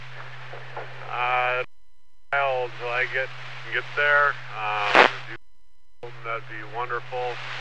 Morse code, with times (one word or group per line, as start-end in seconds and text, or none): none